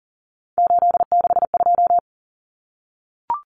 Morse code, heard as a keyed tone, about 40 wpm, 700 Hz. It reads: 862
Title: Numbers - 3-Digit 40wpm